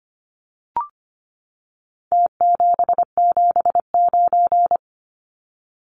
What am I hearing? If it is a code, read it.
Morse code, heard as a keyed tone, about 25 wpm, 700 Hz. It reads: T779
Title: Numbers - Contest Style - 4-Digit 25wpm